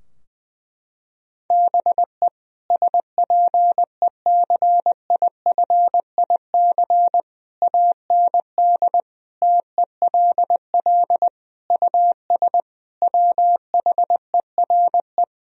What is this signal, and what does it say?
Morse code, heard as a keyed tone, about 20 wpm, 700 Hz. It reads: BE SPECIFIC AND TELL US WHERE
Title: Sentences from Top 1000 Words - Review Part 3 - 20wpm